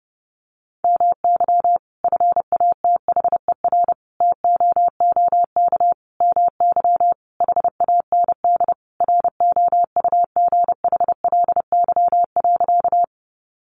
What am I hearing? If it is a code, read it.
Morse code, heard as a keyed tone, about 30 wpm, 700 Hz. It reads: MY FATHER TOOK MY HAND ROUGHLY.
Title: Sentences from Top 1500 Words 30wpm